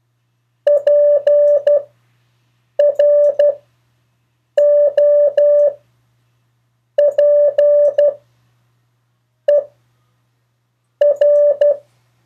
Morse code, heard as a keyed tone, about 12 wpm, spaced as for 6 wpm, 580 Hz. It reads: PROPER
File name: 4of4-sentences-mistakes-test